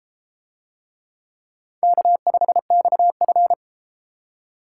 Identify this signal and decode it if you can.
Morse code, heard as a keyed tone, about 32 wpm, 700 Hz. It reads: K5XF